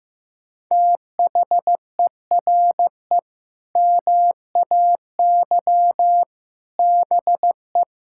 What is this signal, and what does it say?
Morse code, heard as a keyed tone, about 15 wpm, 700 Hz. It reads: THERE MAY BE
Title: Sentences from Top 1000 Words - Review Part 6 - 15wpm